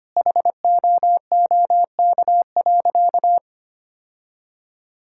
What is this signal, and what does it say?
Morse code, heard as a keyed tone, about 25 wpm, 700 Hz. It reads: HOOK.
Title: Sentences from Top 1600 Words 25wpm